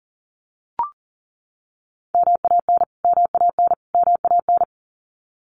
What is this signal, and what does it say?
Morse code, heard as a keyed tone, about 40 wpm, 700 Hz. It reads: MAN MAN MAN
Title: Top 200 Words Repeated 40wpm